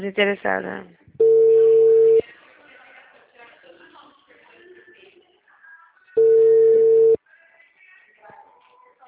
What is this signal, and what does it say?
Ringback tone (China, 450 Hz), with speech.
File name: out- -2004- - - .452